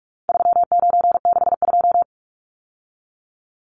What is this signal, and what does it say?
Morse code, heard as a keyed tone, about 44 wpm, 700 Hz. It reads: 3962
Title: Numbers - 4-Digit 45wpm